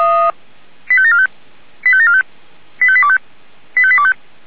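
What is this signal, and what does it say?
Five-tone selective call (ZVEI), heard as quick runs of five tones, tones of about 70 ms. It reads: 85745 85745 85825 85825
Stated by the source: ZVEI five-tone call received on an SDR.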